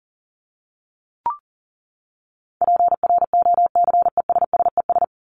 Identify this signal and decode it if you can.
Morse code, heard as a keyed tone, about 40 wpm, 700 Hz. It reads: PROCESSES
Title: ES:S Suffix - Top 100 Words with Base Words 40wpm